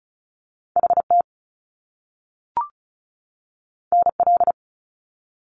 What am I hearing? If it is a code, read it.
Morse code, heard as a keyed tone, about 36 wpm, 700 Hz. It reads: HT NL
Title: Top Two Letter Combinations in Words 35wpm